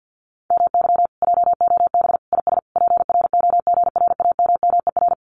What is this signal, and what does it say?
Morse code, heard as a keyed tone, about 50 wpm, 700 Hz. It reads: MY JOB IS PROGRAMMER
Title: Common QSO Phrases Encore 50wpm